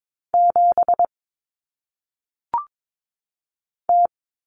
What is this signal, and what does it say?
Morse code, heard as a keyed tone, about 22 wpm, 700 Hz. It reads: 7 T
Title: Common Cut Numbers - 1-Digit 22wpm